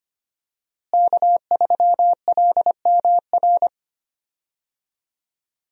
Morse code, heard as a keyed tone, about 25 wpm, 700 Hz. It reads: K3LMR
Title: USA Call Signs - 5 Characters - Repeated 2x - 25wpm